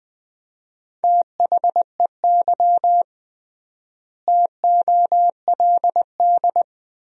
Morse code, heard as a keyed tone, about 20 wpm, 700 Hz. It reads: THEY TOLD
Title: 3x Word-Spacing - Sentences from Top 400 Words - 20wpm